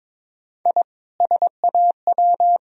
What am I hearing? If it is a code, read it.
Morse code, heard as a keyed tone, about 22 wpm, 700 Hz. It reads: I SAW
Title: Sentences from Top 1000 Words - Review Part 1 - 22wpm